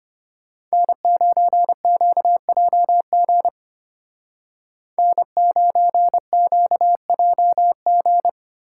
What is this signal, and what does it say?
Morse code, heard as a keyed tone, about 32 wpm, 700 Hz. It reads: N9QJG N9QJG